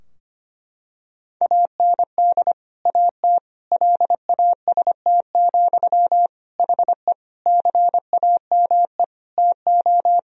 Morse code, heard as a keyed tone, about 25 wpm, 700 Hz. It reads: AND AT LAST, HE CAME TO
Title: Sentences from Top 1000 Words - Review Part 2 - 25wpm